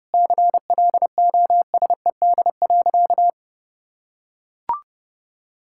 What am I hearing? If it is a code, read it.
Morse code, heard as a keyed tone, about 32 wpm, 700 Hz. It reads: CLOSED.
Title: Sentences from Top 1600 Words 30wpm